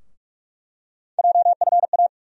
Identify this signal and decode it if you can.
Morse code, heard as a keyed tone, about 46 wpm, 700 Hz. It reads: JFA
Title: USA Call Sign Postfixes - 3 Letters 45wpm